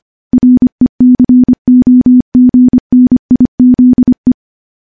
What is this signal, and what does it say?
Morse code, heard as a keyed tone, about 25 wpm, 250 Hz. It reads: RECOGNIZE
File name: morse-training-25WPM